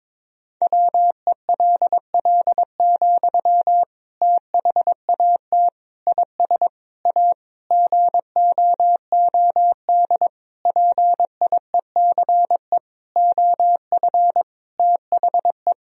Morse code, heard as a keyed tone, about 21 wpm, 700 Hz. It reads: WELL, THAT IS A GOOD PIECE OF THE
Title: Sentences from Top 1000 Words - Review Part 2 - 22wpm